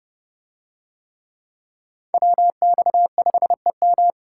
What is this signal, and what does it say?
Morse code, heard as a keyed tone, about 29 wpm, 700 Hz. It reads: WX5EM